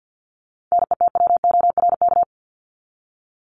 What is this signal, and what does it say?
Morse code, heard as a keyed tone, about 50 wpm, 700 Hz. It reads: NETWORK
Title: Top 751-1000 Words 50wpm